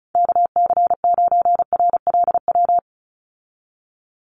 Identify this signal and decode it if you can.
Morse code, heard as a keyed tone, about 34 wpm, 700 Hz. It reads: KC9RLW
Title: USA Call Signs 35wpm